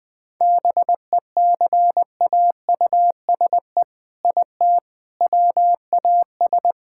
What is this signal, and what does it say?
Morse code, heard as a keyed tone, about 21 wpm, 700 Hz. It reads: BECAUSE IT WAS
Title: All-Age Appropriate Jokes - 20wpm